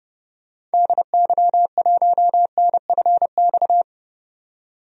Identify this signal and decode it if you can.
Morse code, heard as a keyed tone, about 30 wpm, 700 Hz. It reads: DY1NFX
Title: DXCC-100 Call Signs - 6 Characters - 30wpm